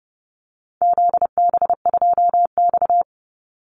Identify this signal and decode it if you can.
Morse code, heard as a keyed tone, about 30 wpm, 700 Hz. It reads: ZB2X
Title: Call Signs Repeated 2x - 30wpm